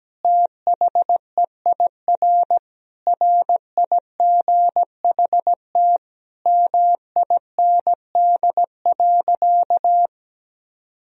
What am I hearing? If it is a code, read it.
Morse code, heard as a keyed tone, about 17 wpm, 700 Hz. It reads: THEIR RIGHT MIND.
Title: Sentences from Top 1000 Words - Review Part 1 - 17wpm